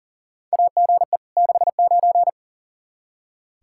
Morse code, heard as a keyed tone, about 37 wpm, 700 Hz. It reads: AGE 69
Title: Practice QSO - Exchange 3 - More Info - 40wpm